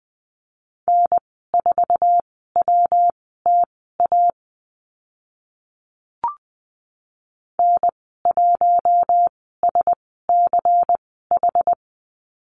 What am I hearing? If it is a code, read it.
Morse code, heard as a keyed tone, about 20 wpm, spaced as for 14 wpm, 700 Hz. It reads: N4WTA N1SCH
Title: 179 Lesson 29g - Slash - Call Signs 20wpm